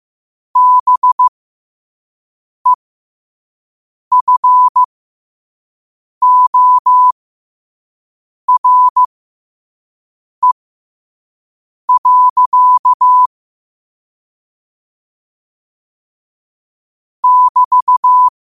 Morse code, heard as a keyed tone, about 15 wpm, spaced as for 5 wpm, 1000 Hz. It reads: BEFORE. <BT>
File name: qotdfiles / 5wpm